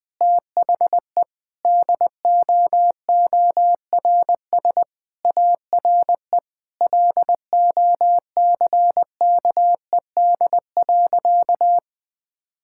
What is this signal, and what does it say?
Morse code, heard as a keyed tone, about 20 wpm, 700 Hz. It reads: THE DOORS ARE LOCKED.
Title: Sentences with 5-Word Phrases - 20wpm